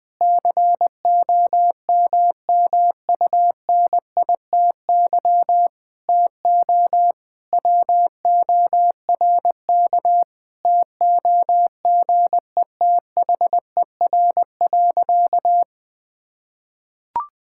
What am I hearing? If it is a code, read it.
Morse code, heard as a keyed tone, about 19 wpm, 700 Hz. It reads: COMMUNITY TO WORK TOGETHER.
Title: Sentences with 7-Word Phrases - Spoken Beginning - 20wpm